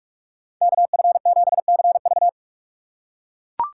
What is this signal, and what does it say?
Morse code, heard as a keyed tone, about 44 wpm, 700 Hz. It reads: KF7CV
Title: USA Call Signs - 5 Characters - 2x Character Spacing - 45wpm